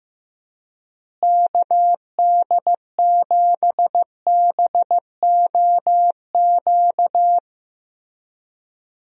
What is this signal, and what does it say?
Morse code, heard as a keyed tone, about 16 wpm, 700 Hz. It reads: KD7BOQ
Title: USA Call Signs - 6 Characters - Repeated 3x - 15wpm